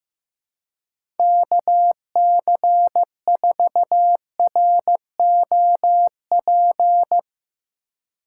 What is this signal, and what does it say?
Morse code, heard as a keyed tone, about 15 wpm, 700 Hz. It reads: KC4ROP
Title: USA Call Signs - 6 Characters - 3x Character Spacing - 15wpm